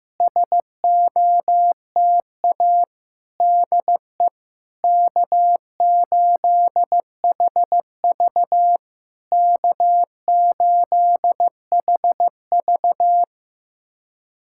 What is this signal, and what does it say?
Morse code, heard as a keyed tone, about 15 wpm, 690 Hz. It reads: SOTA DE K8HV K8HV
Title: SOTA QSO Practice 15wpm